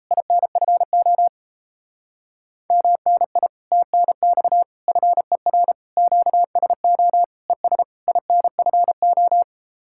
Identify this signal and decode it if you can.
Morse code, heard as a keyed tone, about 37 wpm, 700 Hz. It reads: INFO MNI TNX FER QSO ES INFO